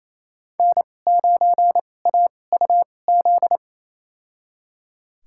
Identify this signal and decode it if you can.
Morse code, heard as a keyed tone, about 28 wpm, spaced as for 20 wpm, 700 Hz. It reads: N9AUZ
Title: USA Call Signs - 2x Character Spacing - 28wpm